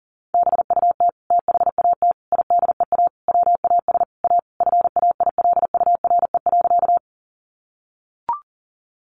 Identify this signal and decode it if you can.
Morse code, heard as a keyed tone, about 40 wpm, 700 Hz. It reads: BUT THAT IDEA WAS A FAILURE.
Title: Sentences from Top 900 Words 40wpm